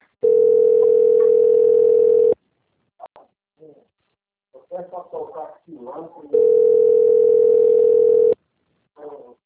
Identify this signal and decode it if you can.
Ringback tone (North America), 440 + 480 Hz, with speech.
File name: out- - - - - .7314